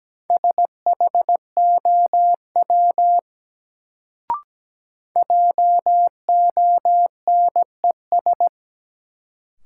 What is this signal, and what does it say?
Morse code, heard as a keyed tone, about 17 wpm, 700 Hz. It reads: SHOW JONES
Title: Sets of 5 Words - New Word Drill - 17wpm